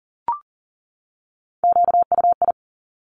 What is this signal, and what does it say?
Morse code, heard as a keyed tone, about 42 wpm, 700 Hz. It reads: QUI
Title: Top Three Letter Combinations in Words 40wpm